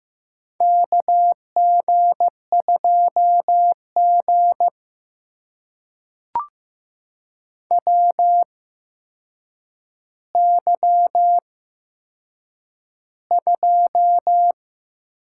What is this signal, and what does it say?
Morse code, heard as a keyed tone, about 15 wpm, spaced as for 7 wpm, 700 Hz. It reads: KG2G WY2
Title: USA Call Signs - 4 Characters - 8x Character Spacing - 15wpm